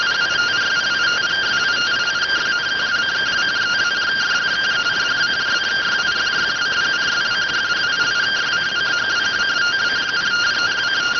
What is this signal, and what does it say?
Two-tone teleprinter signal, about 170 Hz shift.